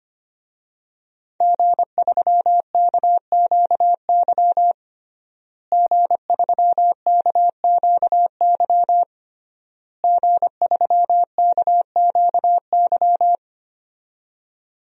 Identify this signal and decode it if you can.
Morse code, heard as a keyed tone, about 25 wpm, 700 Hz. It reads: G3KQY G3KQY G3KQY
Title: DXCC-100 Call Signs - 5 Characters - Repeated 3x - 25wpm